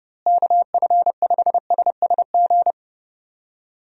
Morse code, heard as a keyed tone, about 32 wpm, 700 Hz. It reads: KF5SSG
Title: USA Call Signs - 6 Characters - Repeated 3x - 30wpm